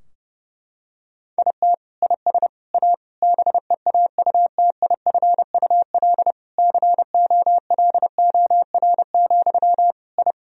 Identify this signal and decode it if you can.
Morse code, heard as a keyed tone, about 30 wpm, 700 Hz. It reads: IT IS A BEAUTIFUL COLOR, I
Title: Sentences from Top 1000 Words - Review Part 2 - 30wpm